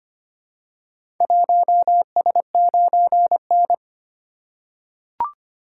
Morse code, heard as a keyed tone, about 25 wpm, 700 Hz. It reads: 1S9N
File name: practice / 25wpm / Four Letter-Number Combinations - 25wpm